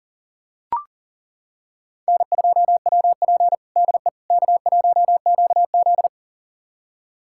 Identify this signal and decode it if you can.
Morse code, heard as a keyed tone, about 40 wpm, 700 Hz. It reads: N2WP DE K1QZ
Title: USA Callsign DE Callsign - 4 Characters - 40wpm